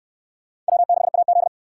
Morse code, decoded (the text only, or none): W6AZ